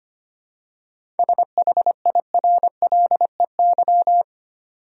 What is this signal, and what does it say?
Morse code, heard as a keyed tone, about 25 wpm, 700 Hz. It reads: SHIRLEY
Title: USA Common Names - 25wpm